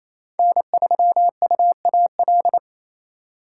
Morse code, heard as a keyed tone, about 27 wpm, 700 Hz. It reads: N3UAL
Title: USA Call Signs - 6x Character Spacing - 28wpm